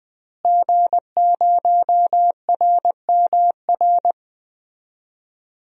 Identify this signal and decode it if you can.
Morse code, heard as a keyed tone, about 20 wpm, 700 Hz. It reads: G0RMR
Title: DXCC-100 Call Signs - 5 Characters - Repeated 3x - 20wpm